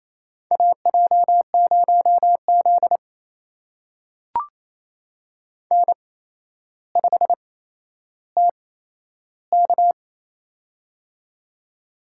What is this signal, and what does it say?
Morse code, heard as a keyed tone, about 28 wpm, spaced as for 12 wpm, 700 Hz. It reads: AJ0Z N5TK
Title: USA Call Signs - 4 Characters - 8x Character Spacing - 28wpm